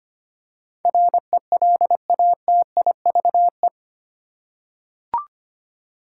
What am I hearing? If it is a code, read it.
Morse code, heard as a keyed tone, about 25 wpm, 700 Hz. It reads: RELATIVE
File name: 8 Letter Words - First 3 Letters Given - 25wpm